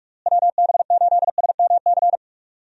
Morse code, heard as a keyed tone, about 46 wpm, 700 Hz. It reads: WB8SMC
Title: USA Call Signs - 6 Characters - Repeated 2x - 45wpm